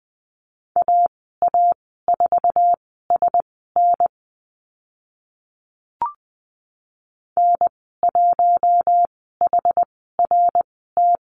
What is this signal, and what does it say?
Morse code, heard as a keyed tone, about 20 wpm, spaced as for 15 wpm, 700 Hz. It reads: AA4SN N1HRT
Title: 062 Lesson 11g - Letter h - Call Signs 20wpm